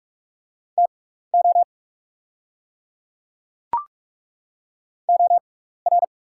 Morse code, heard as a keyed tone, about 43 wpm, spaced as for 15 wpm, 690 Hz. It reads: TO OR
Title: 6x Character Spacing - Top 200 Words - 45wpm